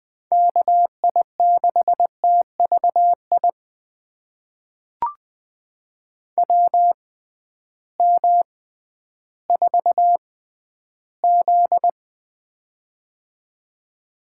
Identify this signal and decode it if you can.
Morse code, heard as a keyed tone, about 20 wpm, spaced as for 12 wpm, 700 Hz. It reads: KI6TVI WM4Z
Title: USA Call Signs - 6x Character Spacing - 20wpm